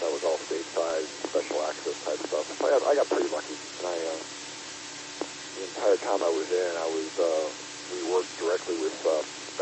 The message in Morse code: none

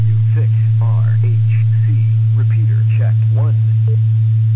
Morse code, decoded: none

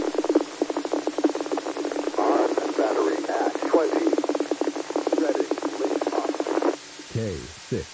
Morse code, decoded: none

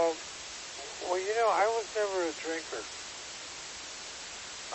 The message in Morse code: none

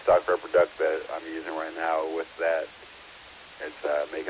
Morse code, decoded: none

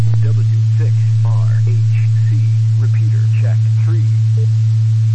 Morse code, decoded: none

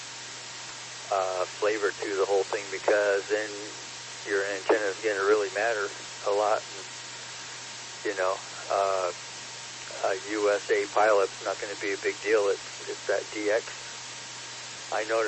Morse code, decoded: none

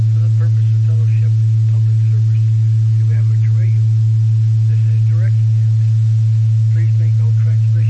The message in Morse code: none

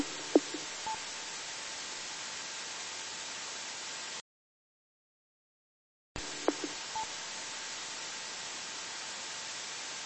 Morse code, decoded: none

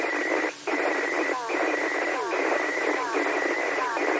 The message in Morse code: none

